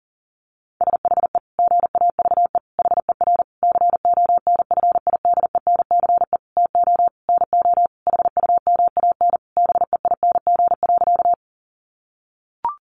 Morse code, heard as a keyed tone, about 40 wpm, 700 Hz. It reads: SHE GAVE HER CONFIDENCE TO NO HUMAN BEING.
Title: Sentences from Top 1500 Words 40wpm